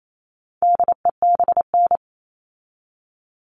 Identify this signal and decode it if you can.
Morse code, heard as a keyed tone, about 28 wpm, 700 Hz. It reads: DEBN
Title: All Cut Numbers - 4-Digit 28wpm